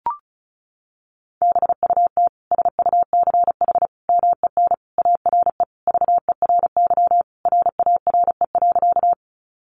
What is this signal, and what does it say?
Morse code, heard as a keyed tone, about 34 wpm, 700 Hz. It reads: BUT SUCH MEN ARE VERY RARE.
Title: Sentences from Top 1100 Words 35wpm